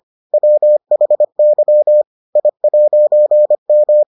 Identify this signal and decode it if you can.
Morse code, heard as a keyed tone, about 25 wpm, 600 Hz. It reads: WHY I'M